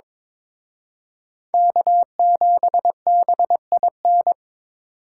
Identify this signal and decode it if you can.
Morse code, heard as a keyed tone, about 26 wpm, 700 Hz. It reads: K7BIN
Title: Speed Racing - USA 1x3 Call Signs 22wpm